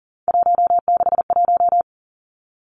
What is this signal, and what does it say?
Morse code, heard as a keyed tone, about 38 wpm, 700 Hz. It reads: 161
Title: Three Challenging Letter-Number Combinations 40wpm